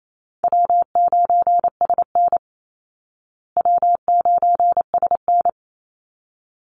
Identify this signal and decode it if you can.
Morse code, heard as a keyed tone, about 28 wpm, 700 Hz. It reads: W9SN W9SN
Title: Call Signs Repeated 3x - 28wpm